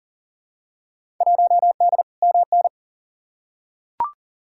Morse code, heard as a keyed tone, about 40 wpm, 700 Hz. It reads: 1D MN
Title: Field Day Exchanges - 40wpm